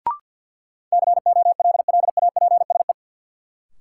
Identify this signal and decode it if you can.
Morse code, heard as a keyed tone, about 47 wpm, 700 Hz. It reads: COLLAPSE
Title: Sets of 5 Words Encore - New Word Drill - 50wpm